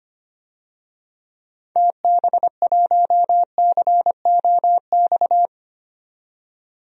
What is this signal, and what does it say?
Morse code, heard as a keyed tone, about 25 wpm, 700 Hz. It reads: TB1COX
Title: DXCC-100 Call Signs - 6 Characters - 25wpm